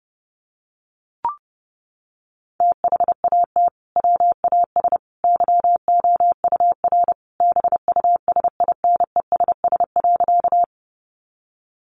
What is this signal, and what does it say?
Morse code, heard as a keyed tone, about 30 wpm, 700 Hz. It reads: THAT WAS YOUR BUSINESS.
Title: Sentences from Top 600 Words 30wpm